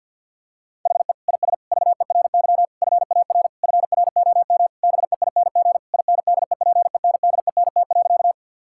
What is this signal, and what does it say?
Morse code, heard as a keyed tone, about 50 wpm, 700 Hz. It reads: HE IS VERY FAR FROM BEING INDEPENDENT.
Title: Sentences from Top 1000 Words - Review Part 6 - 50wpm